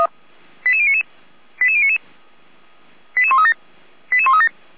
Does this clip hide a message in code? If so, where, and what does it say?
Yes: five-tone selective call (ZVEI) from 0.6 to 4.5 s, heard as quick runs of five tones, tones of about 70 ms. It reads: 88099 88099 88137 88137